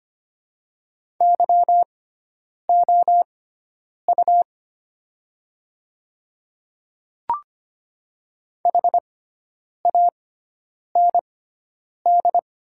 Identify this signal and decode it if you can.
Morse code, heard as a keyed tone, about 25 wpm, spaced as for 9 wpm, 690 Hz. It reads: YOU HAND
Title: 6x Character Spacing - Top 300 Words - 25wpm